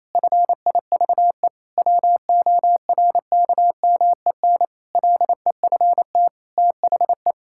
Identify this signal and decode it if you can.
Morse code, heard as a keyed tone, about 27 wpm, 700 Hz. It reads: FIVE WORKMEN LEFT THE
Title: Pangram Challenge v2 - 28wpm